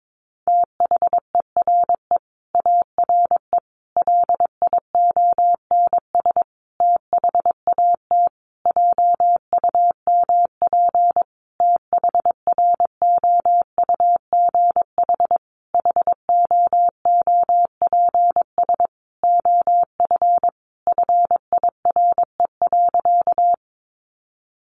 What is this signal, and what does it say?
Morse code, heard as a keyed tone, about 22 wpm, 700 Hz. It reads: THERE ARE LIONS THAT JUMP THROUGH HOOPS OF FIRE.
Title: Easy Phrases 22wpm